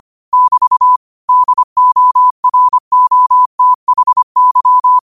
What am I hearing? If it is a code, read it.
Morse code, heard as a keyed tone, about 25 wpm, 1000 Hz. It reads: <BT> DOROTHY
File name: qotdfiles / 25wpm